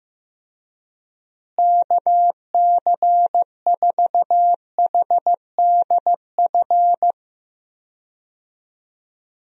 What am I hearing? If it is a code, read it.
Morse code, heard as a keyed tone, about 15 wpm, 700 Hz. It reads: KC4HDF
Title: USA Call Signs - 6 Characters - Repeated 3x - 15wpm